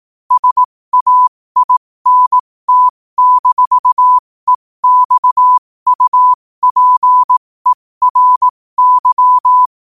Morse code, heard as a keyed tone, about 18 wpm, spaced as for 15 wpm, 1000 Hz. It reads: SAINT-EXUPERY
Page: New quotes every day in morse code at 15 Words per minute.